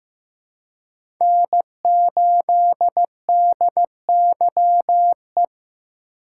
Morse code, heard as a keyed tone, about 15 wpm, 700 Hz. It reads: N8DYE
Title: USA Call Signs - 5 Characters - 6x Character Spacing - 15wpm